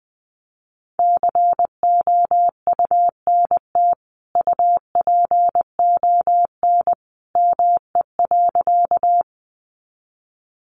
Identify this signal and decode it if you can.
Morse code, heard as a keyed tone, about 20 wpm, 700 Hz. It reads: COUNT UPON ME.
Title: Sentences from Top 1500 Words 20wpm